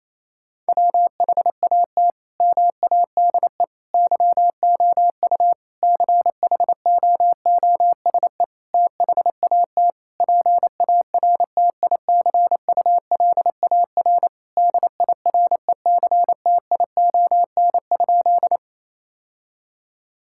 Morse code, heard as a keyed tone, about 27 wpm, 700 Hz. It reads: WHAT MADE YOU CHOOSE THAT PARTICULAR DIRECTION?
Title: Sentences from Top 1000 Words - Review Part 5 - 28wpm